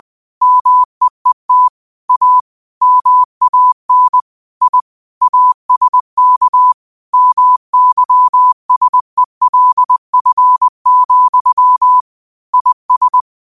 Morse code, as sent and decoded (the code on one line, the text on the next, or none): -- . - / .- / -- .- -. / .. / .- ... -.- / -- -.-- ... .-.. ..-. --..-- / .. ...
MEET A MAN I ASK MYSELF, IS